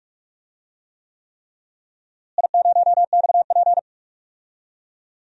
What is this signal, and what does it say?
Morse code, heard as a keyed tone, about 42 wpm, 700 Hz. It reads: I0XP
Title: DXCC-100 Call Signs - 4 Characters - Repeated 3x - 45wpm